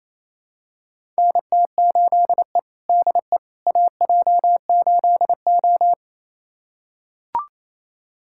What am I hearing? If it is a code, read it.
Morse code, heard as a keyed tone, about 28 wpm, 700 Hz. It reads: NT8E DE AJ8O
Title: USA Callsign DE Callsign - 4 Characters - 28wpm